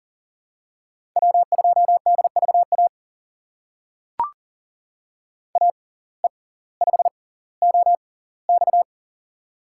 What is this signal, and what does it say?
Morse code, heard as a keyed tone, about 40 wpm, spaced as for 20 wpm, 700 Hz. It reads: W2DVA AE5OX
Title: USA Call Signs - 5 Characters - 6x Character Spacing - 40wpm